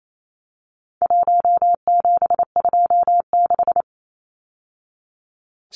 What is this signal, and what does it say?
Morse code, heard as a keyed tone, about 28 wpm, 700 Hz. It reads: 1726
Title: Numbers - 4-Digit 28wpm